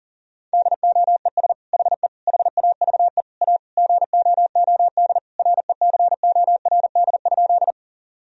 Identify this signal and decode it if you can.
Morse code, heard as a keyed tone, about 40 wpm, 700 Hz. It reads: DOES HE HAVE A GOOD RECORD?
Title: Sentences from Top 1000 Words - Review Part 3 - 40wpm